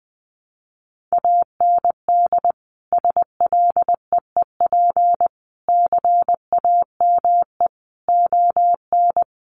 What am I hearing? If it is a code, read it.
Morse code, heard as a keyed tone, about 20 wpm, 700 Hz. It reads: AND SLEEP CAME ON
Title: Sentences from Top 1500 Words 20wpm